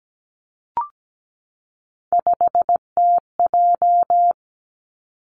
Morse code, heard as a keyed tone, about 17 wpm, 700 Hz. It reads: 5TJ
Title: Three Challenging Letter-Number Combinations 17wpm